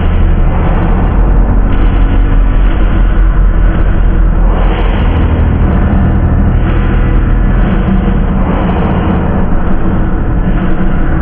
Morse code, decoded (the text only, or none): none